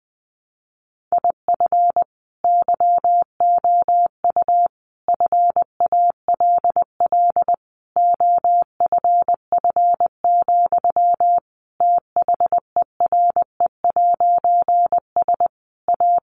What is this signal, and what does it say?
Morse code, heard as a keyed tone, about 20 wpm, 700 Hz. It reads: IF YOU FALL OFF, THERE'S A
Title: A Sound Of Thunder 20wpm